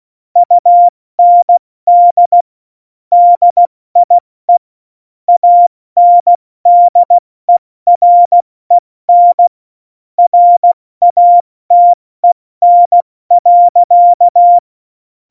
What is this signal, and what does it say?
Morse code, heard as a keyed tone, about 16 wpm, 700 Hz. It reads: UND DIE ANDEREN RATEN.
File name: Morsekurs / Witze / Witz 19. Januar 2021, 70 F80